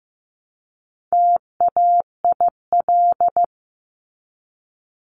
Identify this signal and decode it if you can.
Morse code, heard as a keyed tone, about 15 wpm, 700 Hz. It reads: TAIL
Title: Top 500 Words 15wpm